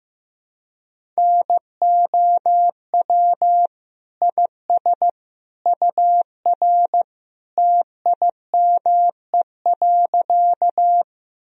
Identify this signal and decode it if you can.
Morse code, heard as a keyed tone, about 15 wpm, 700 Hz. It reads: NOW IS UR TIME.
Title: Sentences from Top 100 Words and Abbreviations 15wpm